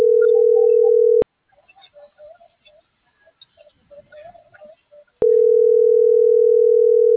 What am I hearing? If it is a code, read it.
Ringback tone (North America), 440 + 480 Hz.